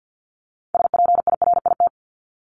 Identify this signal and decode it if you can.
Morse code, heard as a keyed tone, about 50 wpm, 700 Hz. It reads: SPIRIT
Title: Top 1251-1500 Words 50wpm